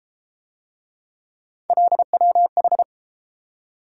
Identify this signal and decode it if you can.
Morse code, heard as a keyed tone, about 31 wpm, 700 Hz. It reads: LWH